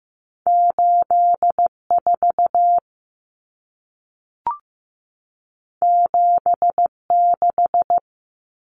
Morse code, heard as a keyed tone, about 15 wpm, 700 Hz. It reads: 84 76
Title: Numbers - 2-Digit 15wpm